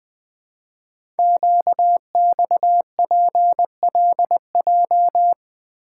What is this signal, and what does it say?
Morse code, heard as a keyed tone, about 20 wpm, 700 Hz. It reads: QXPLJ
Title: Code Groups - 5 Letters - 20wpm